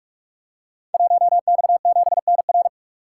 Morse code, heard as a keyed tone, about 44 wpm, 700 Hz. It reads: 1X7NR
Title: Code Groups - 5 Letter:Numbers - 45wpm